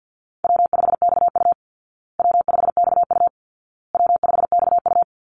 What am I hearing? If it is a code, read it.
Morse code, heard as a keyed tone, about 50 wpm, 700 Hz. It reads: W5XU W5XU W5XU
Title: Easy Call Signs Repeated 3x - 50wpm